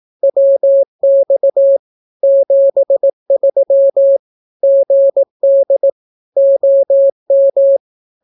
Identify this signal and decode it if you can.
Morse code, heard as a keyed tone, about 18 wpm, 550 Hz. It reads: WX 73 GD OM
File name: naukapodstaw / skroty / skroty